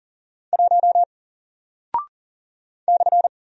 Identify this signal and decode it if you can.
Morse code, heard as a keyed tone, about 40 wpm, 700 Hz. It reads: 1 /
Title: Single Character (Letter:Number:Punctuation:Prosign) 40wpm